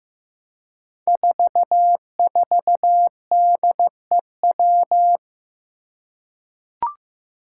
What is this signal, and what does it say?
Morse code, heard as a keyed tone, about 15 wpm, 700 Hz. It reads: 44DEW
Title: Five Challenging Letter-Number Combinations 15wpm